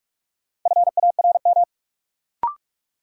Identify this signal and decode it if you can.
Morse code, heard as a keyed tone, about 42 wpm, 700 Hz. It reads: FARM
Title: Sentences from Top 1000 Words - Review Part 2 - New Word Drill - 45wpm